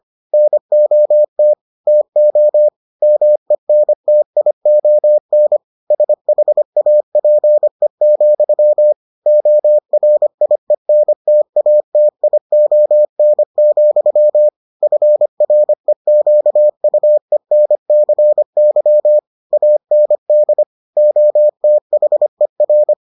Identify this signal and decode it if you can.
Morse code, heard as a keyed tone, about 25 wpm, 600 Hz. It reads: NOT TO MENTION SHAPE, ORIENTATION, FREQUENCY AND OTHER